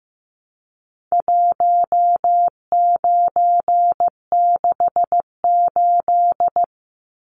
Morse code, heard as a keyed tone, about 15 wpm, 700 Hz. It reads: 1968
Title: Numbers - 4-Digit 15wpm